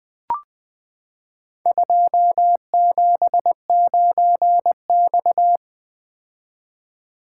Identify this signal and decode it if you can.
Morse code, heard as a keyed tone, about 20 wpm, 700 Hz. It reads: 279X
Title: Four Letter-Number Combinations - 20wpm